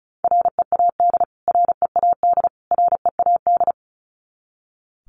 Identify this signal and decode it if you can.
Morse code, heard as a keyed tone, about 34 wpm, 700 Hz. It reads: READ READ READ
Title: Top 200 Words Repeated 35wpm